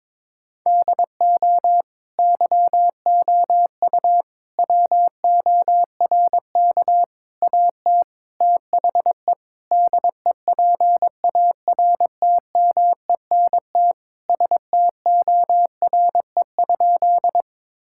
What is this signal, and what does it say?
Morse code, heard as a keyed tone, about 21 wpm, 700 Hz. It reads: DO YOU WORK AT THE DEPARTMENT STORE?
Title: Sentences from Top 1000 Words - Review Part 5 - 22wpm